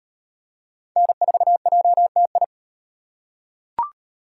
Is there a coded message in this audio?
Yes: Morse code from 1.0 to 2.4 s, heard as a keyed tone, about 38 wpm, 700 Hz. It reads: N4JTI